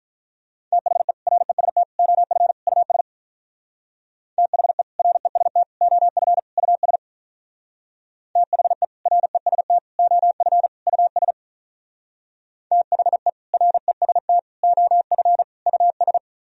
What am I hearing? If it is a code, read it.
Morse code, heard as a keyed tone, about 44 wpm, 700 Hz. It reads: THE REST OF US THE REST OF US THE REST OF US THE REST OF US
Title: Speed-Racing - Sets of 4 Words 35wpm